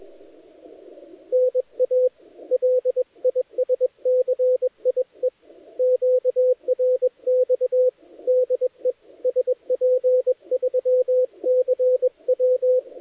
Morse code, heard as a keyed tone, about 21 wpm, 510 Hz, with radio noise. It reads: NA LISCIE QRX DE SP3CW
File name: dlazaawansowanych / nagraniazpasma / titawaNETcontrol